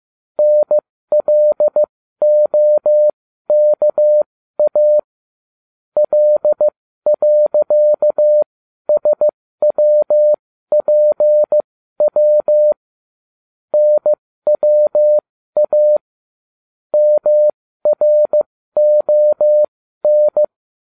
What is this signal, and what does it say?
Morse code, heard as a keyed tone, about 15 wpm, spaced as for 12 wpm, 600 Hz. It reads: NLOKA L.SWPW NWA MRON